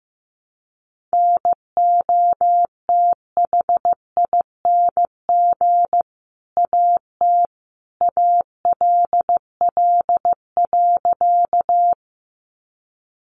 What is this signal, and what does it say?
Morse code, heard as a keyed tone, about 15 wpm, 700 Hz. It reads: NOTHING AT ALL.
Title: Sentences from Top 500 Words 15wpm